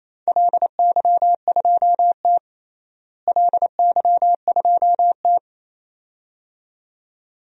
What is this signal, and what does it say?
Morse code, heard as a keyed tone, about 28 wpm, 700 Hz. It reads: LY2T LY2T
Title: DXCC-100 Call Signs - 4 Characters - Repeated 2x - 28wpm